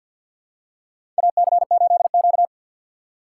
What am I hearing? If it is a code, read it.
Morse code, heard as a keyed tone, about 50 wpm, 700 Hz. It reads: AC8Q
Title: USA Call Signs - 4 Characters - 3x Character Spacing - 50wpm